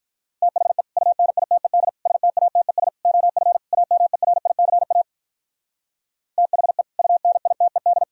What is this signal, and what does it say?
Morse code, heard as a keyed tone, about 43 wpm, 700 Hz. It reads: THE UNITED STATES OF AMERICA THE UNITED
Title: Speed-Racing - Sets of 5 Words 35wpm